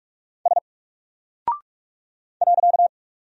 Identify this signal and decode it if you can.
Morse code, heard as a keyed tone, about 44 wpm, 700 Hz. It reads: S .
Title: Single Character (Letter:Number:Punctuation:Prosign) 45wpm